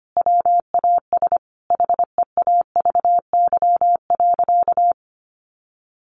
Morse code, heard as a keyed tone, about 25 wpm, 700 Hz. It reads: WAS HEAVY.
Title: Sentences from Top 1100 Words 25wpm